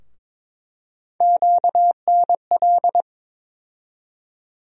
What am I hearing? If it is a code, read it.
Morse code, heard as a keyed tone, about 22 wpm, 700 Hz. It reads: QNL
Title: USA Call Sign Postfixes - 3 Letters 22wpm